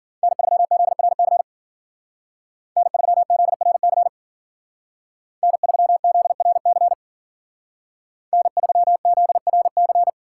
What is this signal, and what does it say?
Morse code, heard as a keyed tone, about 50 wpm, 700 Hz. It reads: N3ZRC N3ZRC N3ZRC N3ZRC